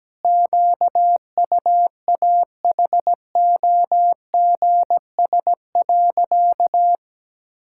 Morse code, heard as a keyed tone, about 17 wpm, 700 Hz. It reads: QUAHOGS.
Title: Pangram Challenge v2 - 17wpm